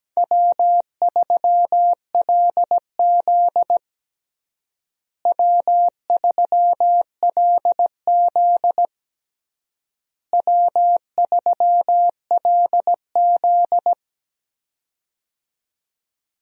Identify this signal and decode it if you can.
Morse code, heard as a keyed tone, about 17 wpm, 700 Hz. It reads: W3LZ W3LZ W3LZ
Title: USA Call Signs - 4 Characters - Repeated 3x - 17wpm